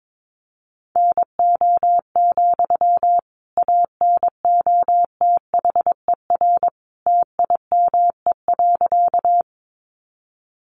Morse code, heard as a keyed tone, about 22 wpm, 700 Hz. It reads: NO, ANOTHER TIME.
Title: Sentences from Top 600 Words 22wpm